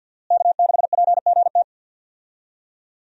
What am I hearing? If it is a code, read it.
Morse code, heard as a keyed tone, about 47 wpm, 700 Hz. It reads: K6PGT
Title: USA Call Signs - 5 Characters - Repeated 3x - 50wpm